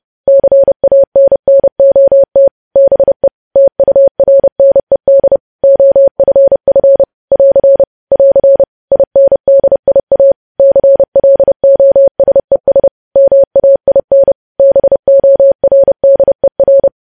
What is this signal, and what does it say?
Morse code, heard as a keyed tone, about 30 wpm, 550 Hz. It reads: CANNOT BE TURNED OFF <AR> <AR> INDIA CLOSES MAIN BORDER
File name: podcast / News headlines-1x-30x30